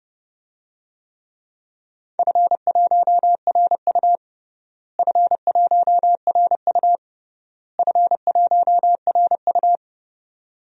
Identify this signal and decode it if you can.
Morse code, heard as a keyed tone, about 32 wpm, 700 Hz. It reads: F1RU F1RU F1RU
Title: DXCC-100 Call Signs - 4 Characters - Repeated 3x - 30wpm